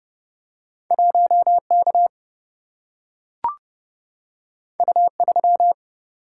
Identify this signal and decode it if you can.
Morse code, heard as a keyed tone, about 30 wpm, 700 Hz. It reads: 1 K U 3
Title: Single-Letter Single-Number 30wpm